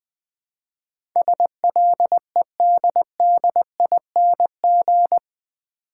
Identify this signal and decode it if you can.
Morse code, heard as a keyed tone, about 20 wpm, 700 Hz. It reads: SLEDDING
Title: Easy Words - Speed-Racing with Hints - 20wpm